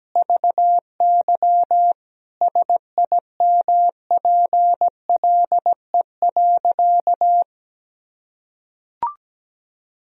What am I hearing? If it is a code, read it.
Morse code, heard as a keyed tone, about 17 wpm, 700 Hz. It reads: VY SIMPLE.
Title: Sentences from Top 500 Words and Abbreviations 17wpm